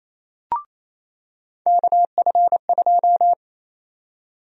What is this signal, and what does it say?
Morse code, heard as a keyed tone, about 28 wpm, 700 Hz. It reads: KF2
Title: USA Call Sign Prefixes - 28wpm